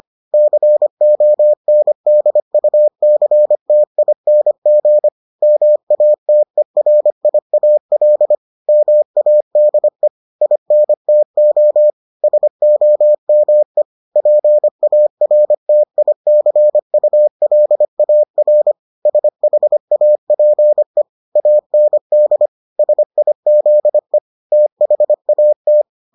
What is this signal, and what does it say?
Morse code, heard as a keyed tone, about 25 wpm, 600 Hz. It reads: CONDUCTING MATERIAL MADE INTO SOME PARTICULAR SHAPE AND SIZE THAT